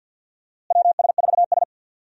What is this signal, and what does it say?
Morse code, heard as a keyed tone, about 50 wpm, 700 Hz. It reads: WS4S
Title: USA Call Signs - 4 Characters - Repeated 2x - 50wpm